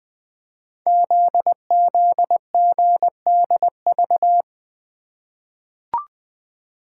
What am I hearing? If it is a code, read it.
Morse code, heard as a keyed tone, about 20 wpm, 700 Hz. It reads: ZZGDV
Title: Code Groups - 5 Letters - 20wpm